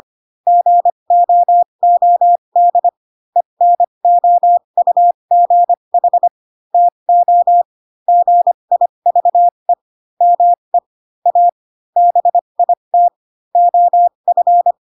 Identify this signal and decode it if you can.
Morse code, heard as a keyed tone, about 25 wpm, spaced as for 22 wpm, 700 Hz. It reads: GOOD ENOUGH TO GIVE ME A BIT OF
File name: Pinocchio / Pinocchio ch6 18 25 ik1hge